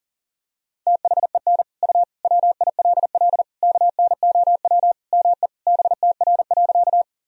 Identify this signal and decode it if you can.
Morse code, heard as a keyed tone, about 40 wpm, 700 Hz. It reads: THEN U WILL KNOW ME BTR.
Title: Sentences from Top 400 Words and Abbreviations 40wpm